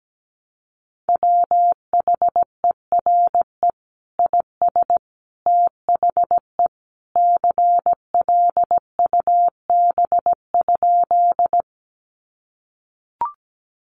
Morse code, heard as a keyed tone, about 17 wpm, 700 Hz. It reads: WHERE IS THE CLUB?
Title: Sentences from Top 1200 Words 17wpm